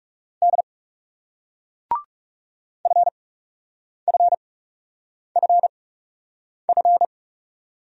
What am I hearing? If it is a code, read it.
Morse code, heard as a keyed tone, about 41 wpm, spaced as for 8 wpm, 700 Hz. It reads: D FFFF